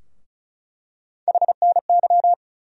Morse code, heard as a keyed tone, about 34 wpm, 700 Hz. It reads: HNY
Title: USA Call Sign Postfixes - 3 Letters 35wpm